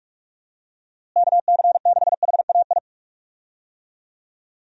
Morse code, heard as a keyed tone, about 46 wpm, 700 Hz. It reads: KC6HAI
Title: USA Call Signs - 6 Characters - Repeated 3x - 45wpm